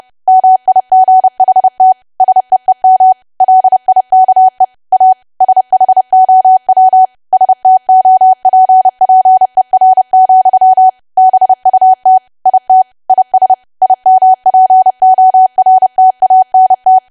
Morse code, heard as a keyed tone, about 30 wpm, 750 Hz. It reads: MIGHT SEEM LIKE A SHOW STOPPER, BUT IT IS IMPORTANT
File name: cwa / W1AW-Level3 / 30WPM